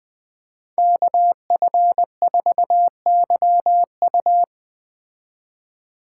Morse code, heard as a keyed tone, about 18 wpm, 700 Hz. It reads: KF4YU
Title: Speed Racing - USA 2x2 Call Signs 17wpm